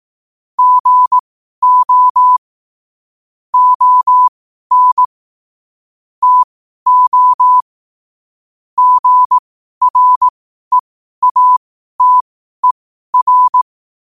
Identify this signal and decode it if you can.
Morse code, heard as a keyed tone, about 18 wpm, spaced as for 12 wpm, 1000 Hz. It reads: GO ON TO GREATER
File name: qotdfiles / 12wpm